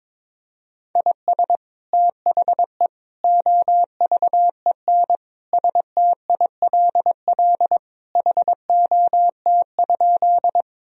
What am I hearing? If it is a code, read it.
Morse code, heard as a keyed tone, about 22 wpm, 700 Hz. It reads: IS THE OVEN STILL HOT?
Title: Sentences from Top 1000 Words - Review Part 4 - 22wpm